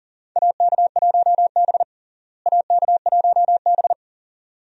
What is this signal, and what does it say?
Morse code, heard as a keyed tone, about 37 wpm, 700 Hz. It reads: AK1B AK1B
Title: USA Call Signs - 4 Characters - Repeated 2x - 40wpm